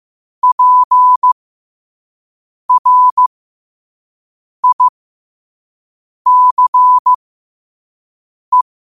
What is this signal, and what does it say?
Morse code, heard as a keyed tone, about 15 wpm, spaced as for 5 wpm, 1000 Hz. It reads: PRICE